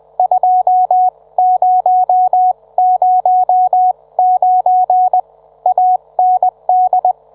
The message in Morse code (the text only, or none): 2009 AND